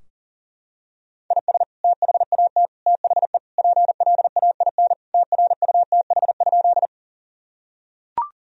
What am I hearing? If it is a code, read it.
Morse code, heard as a keyed tone, about 40 wpm, 700 Hz. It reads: IS THAT THE PLAIN TRUTH?
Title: Sentences from Top 1000 Words - Review Part 2 - 40wpm